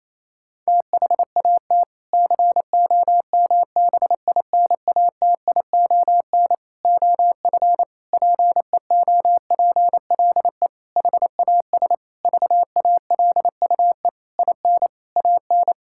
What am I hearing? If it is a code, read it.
Morse code, heard as a keyed tone, about 28 wpm, 700 Hz. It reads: THAT COMBINATION OF PEOPLE HAS VALUE IN AN
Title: Sentences from Top 1000 Words - Review Part 6 - 28wpm